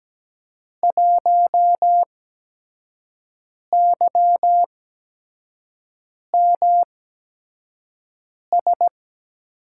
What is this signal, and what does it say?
Morse code, heard as a keyed tone, about 17 wpm, spaced as for 5 wpm, 700 Hz. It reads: 1YMS
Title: USA Call Signs - 5 Characters - 8x Character Spacing - 17wpm